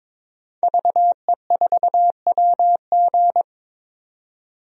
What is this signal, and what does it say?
Morse code, heard as a keyed tone, about 22 wpm, 700 Hz. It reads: VE4WG
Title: DXCC-100 Call Signs - 5 Characters - 22wpm